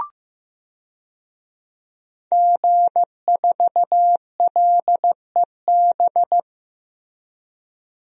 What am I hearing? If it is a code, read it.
Morse code, heard as a keyed tone, about 16 wpm, 700 Hz. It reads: G4LEB
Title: DXCC-100 Call Signs - 5 Characters - Repeated 3x - 15wpm